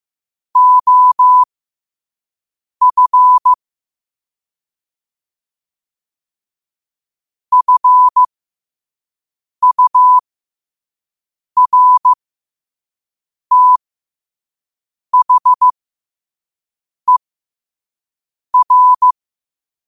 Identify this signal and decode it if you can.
Morse code, heard as a keyed tone, about 15 wpm, spaced as for 5 wpm, 1000 Hz. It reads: OF FURTHER
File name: 5wpm